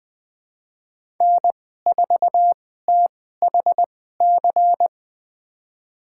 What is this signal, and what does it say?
Morse code, heard as a keyed tone, about 20 wpm, spaced as for 14 wpm, 700 Hz. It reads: N4THC
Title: USA Call Signs - 5 Characters - 2x Character Spacing - 20wpm